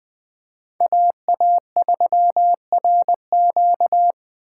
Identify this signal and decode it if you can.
Morse code, heard as a keyed tone, about 20 wpm, 700 Hz. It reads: AA3RQ
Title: USA Call Signs - 5 Characters - 2x Character Spacing - 20wpm